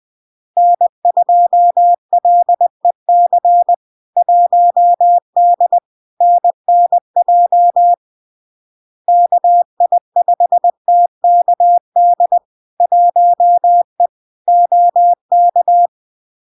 Morse code, heard as a keyed tone, about 20 wpm, 700 Hz. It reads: N2LEC 1D NNJ KI5TKD 1E OK
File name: LARK-FD-2022-Log-in-CW-20wpm-700-Hz